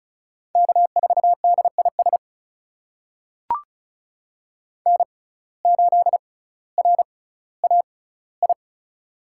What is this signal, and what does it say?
Morse code, heard as a keyed tone, about 36 wpm, spaced as for 18 wpm, 700 Hz. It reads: K4DIS N8RAI
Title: USA Call Signs - 5 Characters - 6x Character Spacing - 35wpm